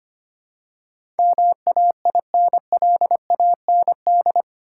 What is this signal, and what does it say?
Morse code, heard as a keyed tone, about 25 wpm, 700 Hz. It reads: MAINLAND
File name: Eight Letter Words - 25wpm